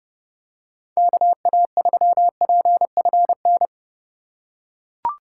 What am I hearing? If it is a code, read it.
Morse code, heard as a keyed tone, about 31 wpm, 700 Hz. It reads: KA3PFN
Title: USA Call Signs - 6 Characters - 4x Character Spacing - 30wpm